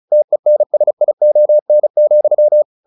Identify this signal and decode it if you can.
Morse code, heard as a keyed tone, about 34 wpm, 600 Hz. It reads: TENSION,
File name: the old man and the sea / the old man and the sea 1 35wpm